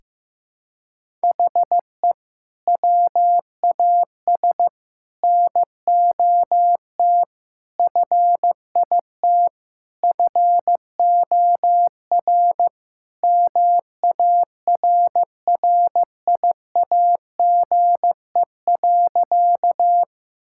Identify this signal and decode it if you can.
Morse code, heard as a keyed tone, about 15 wpm, 690 Hz. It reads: HE WAS NOT FIT FOR MARRIAGE.
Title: Sentences from Top 1000 Words - Review Part 5 - 15wpm